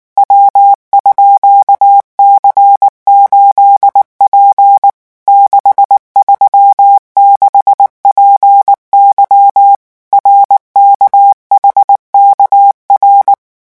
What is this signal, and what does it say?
Morse code, heard as a keyed tone, about 19 wpm, 800 Hz. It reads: W_C8P 636PY RKHKR
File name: morse mp3 / revision12-vitesse 18 mots